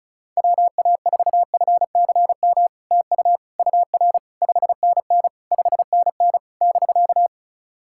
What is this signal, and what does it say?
Morse code, heard as a keyed tone, about 36 wpm, 700 Hz. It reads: WA4FCM TU UR 5NN 5NN <BK>
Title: SOTA QSO Practice 35wpm